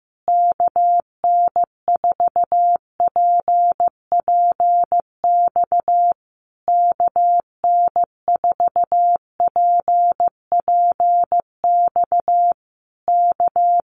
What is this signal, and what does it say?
Morse code, heard as a keyed tone, about 15 wpm, 700 Hz. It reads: KN4PPX KN4PPX K
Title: Common QSO Phrases Encore 15wpm